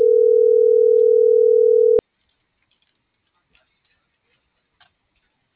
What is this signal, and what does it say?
Ringback tone (North America), 440 + 480 Hz.